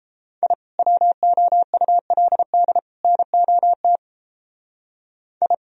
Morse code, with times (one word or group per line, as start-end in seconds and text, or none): 0.4-0.5 I
0.8-2.8 WOULD
3.0-4.0 NOT
5.4-5.5 I